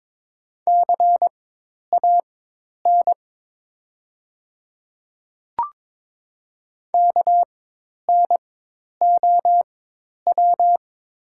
Morse code, heard as a keyed tone, about 22 wpm, spaced as for 10 wpm, 700 Hz. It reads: CAN KNOW
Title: 4x Character Spacing - Top 100 Words - 22wpm